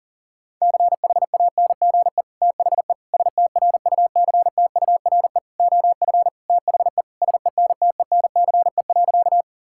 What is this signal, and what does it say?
Morse code, heard as a keyed tone, about 37 wpm, 700 Hz. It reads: CHANGE THE STRUCTURE OF THE SENTENCE.
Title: Sentences from Top 1000 Words - Review Part 6 - 40wpm